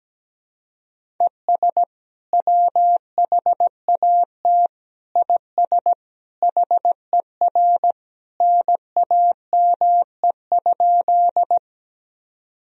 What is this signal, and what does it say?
Morse code, heard as a keyed tone, about 17 wpm, 700 Hz. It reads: ES WHAT IS HER NAME?
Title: Sentences from Top 200 Words and Abbreviations 17wpm